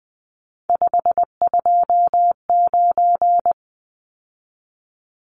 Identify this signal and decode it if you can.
Morse code, heard as a keyed tone, about 20 wpm, 700 Hz. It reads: 529
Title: Numbers - 3-Digit 20wpm